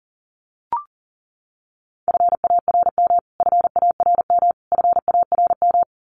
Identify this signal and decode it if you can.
Morse code, heard as a keyed tone, about 40 wpm, 700 Hz. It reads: FARM FARM FARM
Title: Top 500 Words Repeated 40wpm